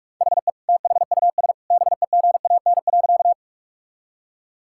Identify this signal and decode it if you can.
Morse code, heard as a keyed tone, about 46 wpm, 700 Hz. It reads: HE THUS BEGAN.
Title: Sentences from Top 1000 Words - Review Part 5 - 45wpm